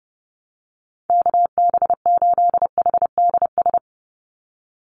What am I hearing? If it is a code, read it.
Morse code, heard as a keyed tone, about 28 wpm, 700 Hz. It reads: KB8HDS
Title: USA Call Signs 30wpm